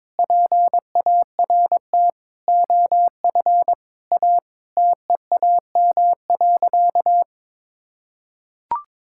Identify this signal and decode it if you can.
Morse code, heard as a keyed tone, about 22 wpm, 700 Hz. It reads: PART OF A TEAM.
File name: practice / 22wpm / Sentences with 4-Word Phrases - 22wpm